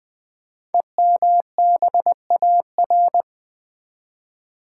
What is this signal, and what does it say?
Morse code, heard as a keyed tone, about 20 wpm, 700 Hz. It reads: EMBAR
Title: Build A Word - 5 Letters At A Time - 12-Letter Words - 20wpm